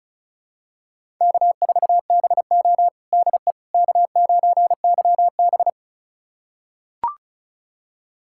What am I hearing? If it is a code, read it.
Morse code, heard as a keyed tone, about 36 wpm, 700 Hz. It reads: K4BO DE K9YB
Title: USA Callsign DE Callsign - 4 Characters - 35wpm